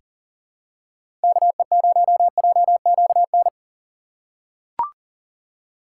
Morse code, heard as a keyed tone, about 40 wpm, 700 Hz. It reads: KE0JQN
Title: USA Call Signs - 6 Characters - 6x Character Spacing - 40wpm